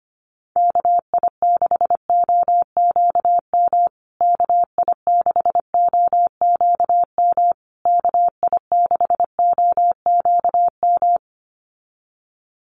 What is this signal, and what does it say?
Morse code, heard as a keyed tone, about 25 wpm, 700 Hz. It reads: KI6OQM KI6OQM KI6OQM
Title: USA Call Signs Repeated 3x 25wpm